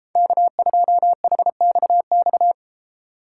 Morse code, heard as a keyed tone, about 32 wpm, 700 Hz. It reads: K2HXX